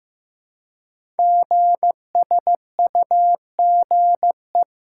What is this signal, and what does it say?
Morse code, heard as a keyed tone, about 15 wpm, 700 Hz. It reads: GSUGE
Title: Code Groups - 5 Letters - 15wpm